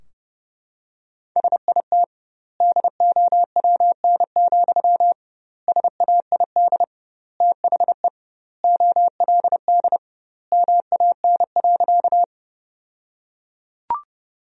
Morse code, heard as a keyed tone, about 32 wpm, 700 Hz. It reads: SIT DOWN, SAID THE OLD MAN.
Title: 2x Word-Spacing - Sentences from Top 500 Words - 30wpm